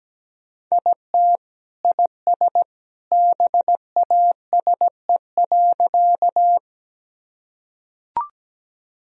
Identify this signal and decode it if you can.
Morse code, heard as a keyed tone, about 17 wpm, 700 Hz. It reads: IT IS BASE.
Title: Sentences from Top 1000 Words - Review Part 1 - 17wpm